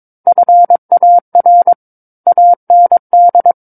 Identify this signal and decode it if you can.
Morse code, heard as a keyed tone, about 22 wpm, 700 Hz. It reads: FAR AND
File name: Book One / 13-Chapter 13 - 22 WPM